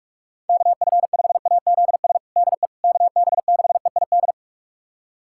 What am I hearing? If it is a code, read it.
Morse code, heard as a keyed tone, about 46 wpm, 700 Hz. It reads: KF5AZS DE KB6EID
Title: USA Callsign DE Callsign - 6 Characters - 45wpm